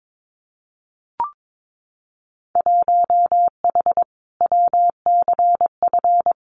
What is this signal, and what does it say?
Morse code, heard as a keyed tone, about 22 wpm, 700 Hz. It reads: 1H WCF
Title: Winter Field Day Exchanges 22wpm